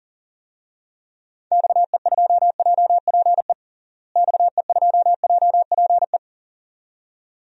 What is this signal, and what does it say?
Morse code, heard as a keyed tone, about 40 wpm, 700 Hz. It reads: XE2JPE XE2JPE
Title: DXCC-100 Call Signs - 6 Characters - Repeated 2x - 40wpm